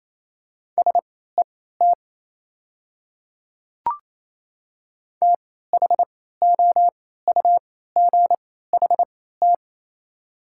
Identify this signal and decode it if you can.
Morse code, heard as a keyed tone, about 28 wpm, spaced as for 16 wpm, 700 Hz. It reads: SET THOUGHT
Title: 3x Character Spacing - Top 300 Words - 28wpm